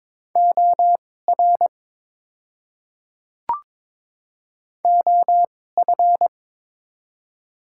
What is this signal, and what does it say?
Morse code, heard as a keyed tone, about 23 wpm, spaced as for 16 wpm, 700 Hz. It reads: OR OF
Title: 2x Character Spacing - Top 100 Words - 22wpm